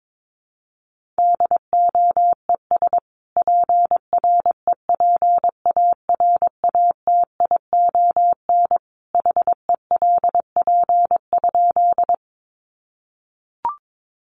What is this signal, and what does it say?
Morse code, heard as a keyed tone, about 22 wpm, 700 Hz. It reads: DOES PREPARATION HELP?
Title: Sentences from Top 1400 Words 22wpm